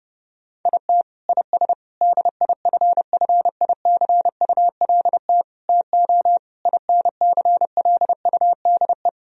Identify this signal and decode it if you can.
Morse code, heard as a keyed tone, about 32 wpm, 700 Hz. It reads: IT IS DIFFICULT TO INCLUDE
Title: Sentences from Top 1000 Words - Review Part 3 - 30wpm